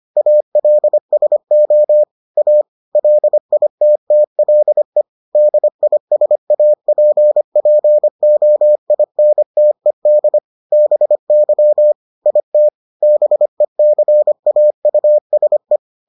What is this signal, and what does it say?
Morse code, heard as a keyed tone, about 25 wpm, 600 Hz. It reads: ALSO A LITTLE DISAPPOINTED BY IT BECAUSE